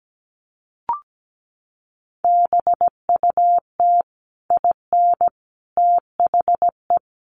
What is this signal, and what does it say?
Morse code, heard as a keyed tone, about 17 wpm, 700 Hz. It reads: BUT IN THE
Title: Sentences from Top 1500 Words 17wpm